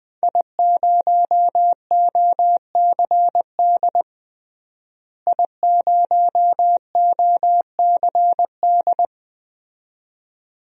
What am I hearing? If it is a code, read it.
Morse code, heard as a keyed tone, about 21 wpm, 700 Hz. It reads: I0OCD I0OCD
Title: DXCC-100 Call Signs - 5 Characters - Repeated 2x - 20wpm